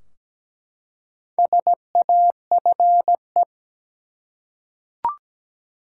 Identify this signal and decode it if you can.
Morse code, heard as a keyed tone, about 17 wpm, 700 Hz. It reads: SAFE
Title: Four Letter Words - 17wpm